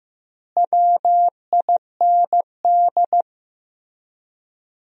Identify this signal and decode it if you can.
Morse code, heard as a keyed tone, about 15 wpm, 700 Hz. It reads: WIND
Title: 3x Word-Spacing - Sentences from Top 400 Words - 15wpm